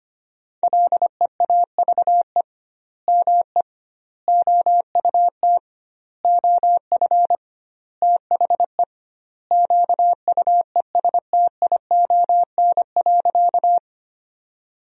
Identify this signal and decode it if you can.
Morse code, heard as a keyed tone, about 25 wpm, 700 Hz. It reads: LEAVE ME OUT OF THE QUESTION.
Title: 2x Word-Spacing - Sentences from Top 400 Words - 25wpm